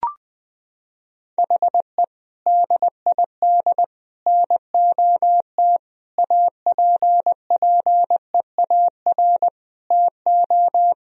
Morse code, heard as a keyed tone, about 21 wpm, 700 Hz. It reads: HE DID NOT APPEAR TO
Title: Sentences from Top 1000 Words - Review Part 2 - 20wpm